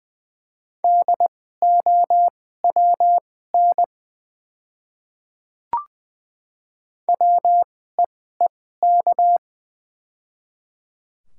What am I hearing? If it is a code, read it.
Morse code, heard as a keyed tone, about 20 wpm, spaced as for 14 wpm, 690 Hz. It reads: DOWN WEEK
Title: 2x Character Spacing - Top 500 Words - 20wpm